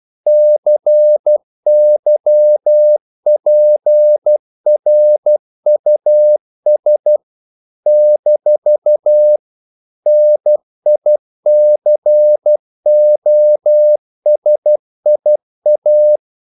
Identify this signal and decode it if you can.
Morse code, heard as a keyed tone, about 12 wpm, 600 Hz. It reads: CYPRUS - NICOSIA
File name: CW / Practice / NationalCapitals12